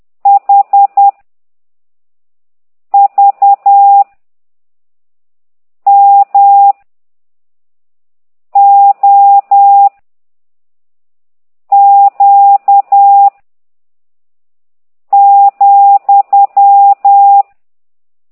Morse code, heard as a keyed tone, about 10 wpm, 800 Hz. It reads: H V M O Q ,